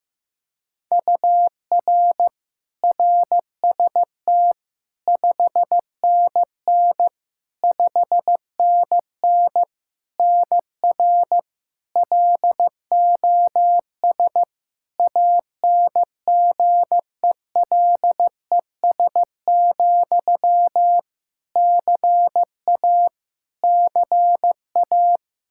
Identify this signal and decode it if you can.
Morse code, heard as a keyed tone, about 16 wpm, 700 Hz. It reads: UR RST 5NN 5NN NR LOS ANGELES, CA CA
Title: K1USN SST QSO Practice 15wpm